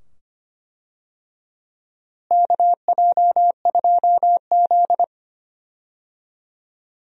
Morse code, heard as a keyed tone, about 25 wpm, 700 Hz. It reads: KJ2Z
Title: USA Call Signs - 4 Characters - Repeated 2x - 25wpm